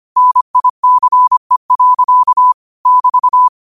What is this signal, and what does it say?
Morse code, heard as a keyed tone, about 25 wpm, 1000 Hz. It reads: NICE. <BT>
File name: qotdfiles / 25wpm